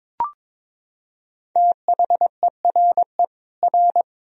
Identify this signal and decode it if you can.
Morse code, heard as a keyed tone, about 22 wpm, 700 Hz. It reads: THERE R
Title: Sentences from Top 200 Words and Abbreviations 22wpm